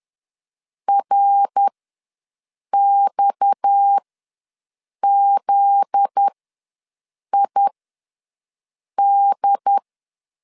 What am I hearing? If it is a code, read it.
Morse code, heard as a keyed tone, about 10 wpm, spaced as for 6 wpm, 800 Hz. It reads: RXZID
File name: LEZIONI / 7 GF9Z / 7 GF9Z ws 10wpm 3919